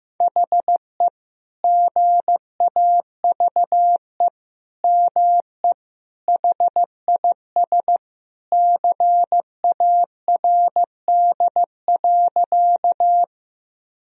Morse code, heard as a keyed tone, about 15 wpm, 700 Hz. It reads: HE GAVE ME HIS CARD.
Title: Sentences from Top 1000 Words - Review Part 4 - 15wpm